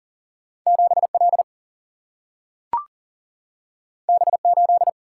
Morse code, heard as a keyed tone, about 40 wpm, 700 Hz. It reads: 7L B8
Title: Single-Letter Single-Number 40wpm